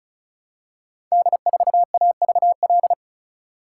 Morse code, heard as a keyed tone, about 35 wpm, 700 Hz. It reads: D4AVL
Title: Code Groups - 5 Letter:Numbers - 35wpm